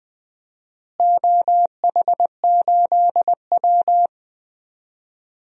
Morse code, heard as a keyed tone, about 20 wpm, 700 Hz. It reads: OH8W
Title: DXCC-100 Call Signs - 4 Characters - Repeated 2x - 20wpm